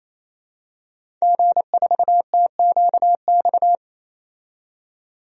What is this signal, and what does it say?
Morse code, heard as a keyed tone, about 28 wpm, 700 Hz. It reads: G4TQX
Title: DXCC-100 Call Signs - 5 Characters - 28wpm